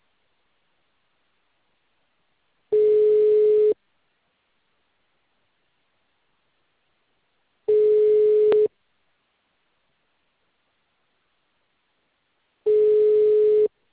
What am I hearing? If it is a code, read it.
Ringback tone (Europe, 425 Hz).